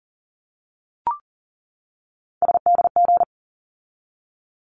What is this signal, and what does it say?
Morse code, heard as a keyed tone, about 40 wpm, 700 Hz. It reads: SDG
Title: ARRL Sections 40wpm